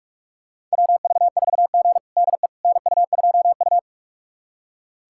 Morse code, heard as a keyed tone, about 46 wpm, 700 Hz. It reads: WV4G DE NU2U
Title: USA Callsign DE Callsign - 4 Characters - 45wpm